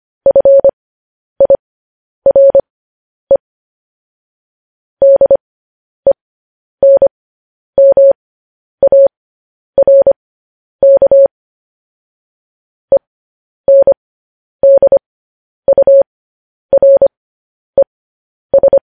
Morse code, heard as a keyed tone, about 25 wpm, spaced as for 10 wpm, 550 Hz. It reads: FIRE DENMARK ENDURES